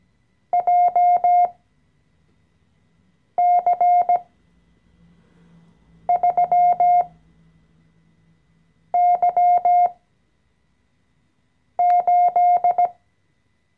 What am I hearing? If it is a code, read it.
Morse code, heard as a keyed tone, about 17 wpm, spaced as for 4 wpm, 710 Hz. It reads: JC3Y8
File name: CW-Online-Kurs / Anfänger / Lektion 014